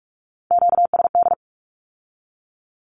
Morse code, heard as a keyed tone, about 46 wpm, 700 Hz. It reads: QSD
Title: Q-Codes with Spoken Meaning - 45wpm